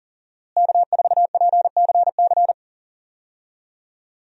Morse code, heard as a keyed tone, about 40 wpm, 700 Hz. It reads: K4PCC
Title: USA Call Signs - 5 Characters - Repeated 3x - 40wpm